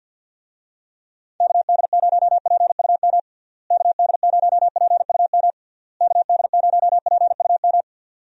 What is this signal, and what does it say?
Morse code, heard as a keyed tone, about 52 wpm, 700 Hz. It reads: KD0PUM KD0PUM KD0PUM
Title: USA Call Signs - 6 Characters - Repeated 3x - 50wpm